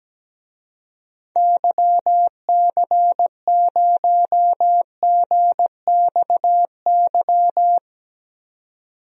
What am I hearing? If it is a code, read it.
Morse code, heard as a keyed tone, about 17 wpm, 700 Hz. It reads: YC0GXY
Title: DXCC-100 Call Signs - 6 Characters - 17wpm